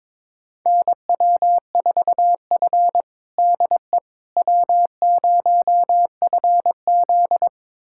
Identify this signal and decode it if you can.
Morse code, heard as a keyed tone, about 22 wpm, 700 Hz. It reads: NW4F DE W0FZ
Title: USA Callsign DE Callsign - 4 Characters - 22wpm